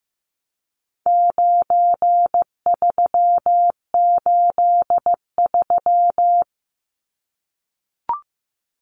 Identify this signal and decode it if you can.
Morse code, heard as a keyed tone, about 15 wpm, 700 Hz. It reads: 9383
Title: Numbers - 4-Digit 15wpm